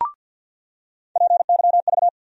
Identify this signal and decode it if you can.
Morse code, heard as a keyed tone, about 54 wpm, 700 Hz. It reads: PYV
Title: USA Call Sign Postfixes - 3 Letters 50wpm